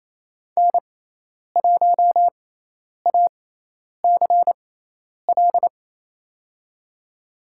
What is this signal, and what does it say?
Morse code, heard as a keyed tone, about 28 wpm, spaced as for 10 wpm, 700 Hz. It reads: N1ACL
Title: USA Call Signs - 5 Characters - 6x Character Spacing - 28wpm